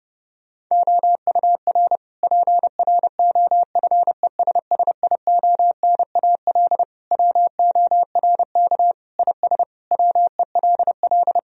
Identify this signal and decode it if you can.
Morse code, heard as a keyed tone, about 32 wpm, 700 Hz. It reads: OUR PROFESSIONAL WORK IS WELL
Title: Sentences from Top 1000 Words - Review Part 4 - 30wpm